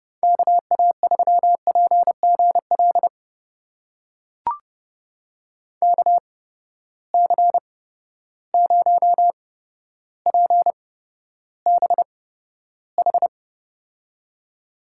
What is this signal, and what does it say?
Morse code, heard as a keyed tone, about 31 wpm, spaced as for 13 wpm, 700 Hz. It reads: KA3PGL KC0PBH
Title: USA Call Signs - 6 Characters - 8x Character Spacing - 30wpm